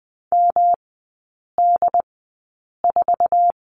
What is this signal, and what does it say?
Morse code, heard as a keyed tone, about 20 wpm, spaced as for 8 wpm, 700 Hz. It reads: MD4
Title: 107 Lesson 18e - Letter m - Three Characters 20wpm